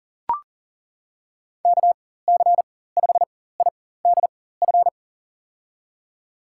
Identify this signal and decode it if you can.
Morse code, heard as a keyed tone, about 40 wpm, spaced as for 19 wpm, 700 Hz. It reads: KC5IDF
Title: USA Call Signs - 6 Characters - 4x Character Spacing - 40wpm